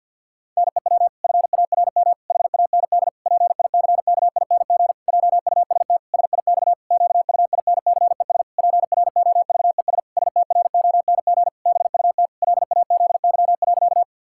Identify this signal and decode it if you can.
Morse code, heard as a keyed tone, about 50 wpm, 700 Hz. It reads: NEW FARM HAND PICKING JUST SIX QUINCES PROVES STRONG BUT LAZY.
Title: Pangram Challenge v2 - 50wpm